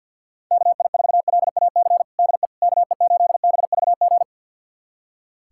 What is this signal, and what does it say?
Morse code, heard as a keyed tone, about 46 wpm, 700 Hz. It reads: KI4LAC DE KE8BVG
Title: USA Callsign DE Callsign - 6 Characters - 50wpm